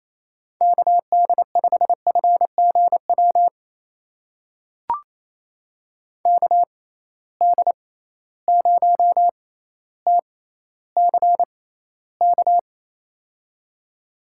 Morse code, heard as a keyed tone, about 28 wpm, spaced as for 14 wpm, 700 Hz. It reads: KD5FGW KD0TCK
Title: USA Call Signs - 6 Characters - 6x Character Spacing - 28wpm